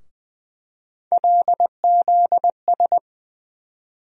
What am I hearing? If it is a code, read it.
Morse code, heard as a keyed tone, about 20 wpm, 700 Hz. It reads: LZS
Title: USA Call Sign Postfixes - 3 Letters 20wpm